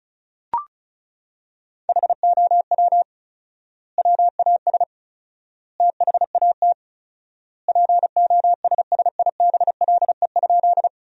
Morse code, heard as a keyed tone, about 36 wpm, 700 Hz. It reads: HOW WAS THAT POSSIBLE?
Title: 4x Word-Spacing - Sentences from Top 500 Words - 35wpm